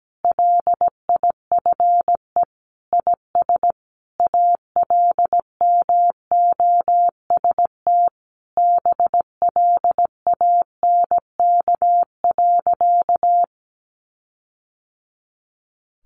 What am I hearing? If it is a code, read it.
Morse code, heard as a keyed tone, about 17 wpm, 700 Hz. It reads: LIFE IS ALMOST BLANK.
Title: Sentences from Top 1500 Words 17wpm